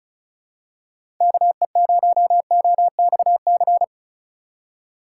Morse code, heard as a keyed tone, about 34 wpm, 700 Hz. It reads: KE0OXC
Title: USA Call Signs - 6 Characters - 8x Character Spacing - 35wpm